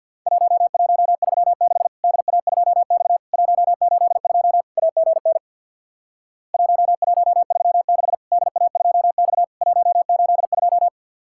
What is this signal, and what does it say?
Morse code, heard as a keyed tone, about 50 wpm, 700 Hz. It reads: 1136 DA2X 182 AGN 1136 DA2X 182
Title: Worked All Europe DX Contest - QTC Exchange with Repeats - 50wpm